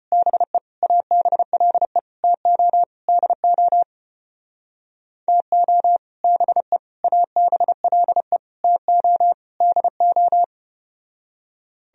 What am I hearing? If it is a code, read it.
Morse code, heard as a keyed tone, about 33 wpm, 700 Hz. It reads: BE ABLE TO DO TO BE ABLE TO DO